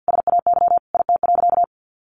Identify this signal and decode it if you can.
Morse code, heard as a keyed tone, about 50 wpm, 700 Hz. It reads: SAY IT.
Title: Sentences from Top 1000 Words 50wpm